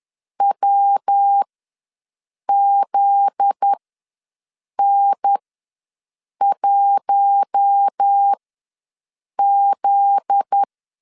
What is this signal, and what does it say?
Morse code, heard as a keyed tone, about 10 wpm, spaced as for 6 wpm, 800 Hz. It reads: WZN1Z
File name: LEZIONI / 7 GF9Z / 7 GF9Z ws 10wpm 3919